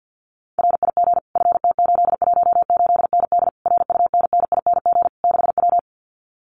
Morse code, heard as a keyed tone, about 50 wpm, 700 Hz. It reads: RIG FT818ND RUNNING 6W
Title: Common QSO Phrases Encore 50wpm